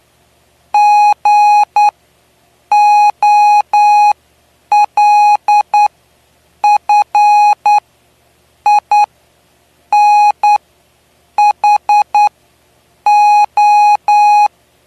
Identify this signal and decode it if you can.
Morse code, heard as a keyed tone, about 9 wpm, spaced as for 7 wpm, 830 Hz. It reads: GOLFINHO
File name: cw / py2lcd / 09-PY2LCD-Li��o 9